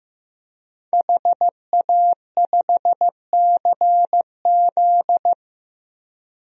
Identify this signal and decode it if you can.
Morse code, heard as a keyed tone, about 16 wpm, 700 Hz. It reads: HA5CZ
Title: DXCC-100 Call Signs - 5 Characters - 15wpm